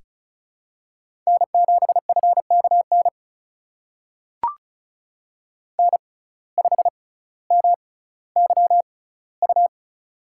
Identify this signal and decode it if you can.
Morse code, heard as a keyed tone, about 36 wpm, spaced as for 18 wpm, 700 Hz. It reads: N7FKN N5MYU
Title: USA Call Signs - 5 Characters - 6x Character Spacing - 35wpm